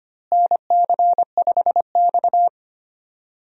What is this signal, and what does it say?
Morse code, heard as a keyed tone, about 25 wpm, 700 Hz. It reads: NC5X
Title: USA Call Signs - 4 Characters - Repeated 3x - 25wpm